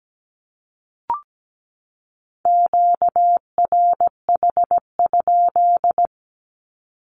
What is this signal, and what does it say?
Morse code, heard as a keyed tone, about 17 wpm, 700 Hz. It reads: QRH?
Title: Q-Codes 17wpm